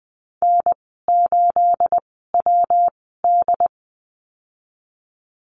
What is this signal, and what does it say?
Morse code, heard as a keyed tone, about 20 wpm, spaced as for 15 wpm, 700 Hz. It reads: N8WD
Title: 238 Lesson 38g - Letter q - Call Signs 20wpm